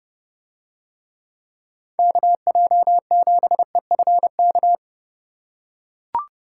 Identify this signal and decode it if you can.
Morse code, heard as a keyed tone, about 30 wpm, 700 Hz. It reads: KJ7EFK